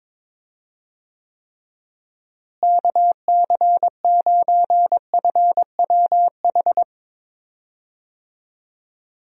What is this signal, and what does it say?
Morse code, heard as a keyed tone, about 22 wpm, 700 Hz. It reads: KC9FWH
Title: USA Call Signs - 6 Characters - Repeated 2x - 22wpm